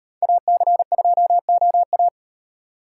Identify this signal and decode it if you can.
Morse code, heard as a keyed tone, about 38 wpm, 700 Hz. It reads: AC2OA